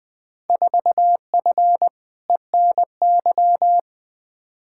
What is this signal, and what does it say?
Morse code, heard as a keyed tone, about 20 wpm, 700 Hz. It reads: 4F ENY
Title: Field Day Exchanges - 20wpm